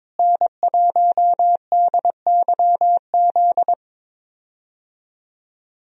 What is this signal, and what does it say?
Morse code, heard as a keyed tone, about 22 wpm, 700 Hz. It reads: N1DYZ
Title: USA Call Signs - 5 Characters - Repeated 3x - 22wpm